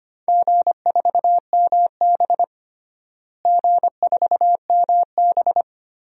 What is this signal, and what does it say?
Morse code, heard as a keyed tone, about 25 wpm, 700 Hz. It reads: G4MB G4MB
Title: DXCC-100 Call Signs - 4 Characters - Repeated 3x - 25wpm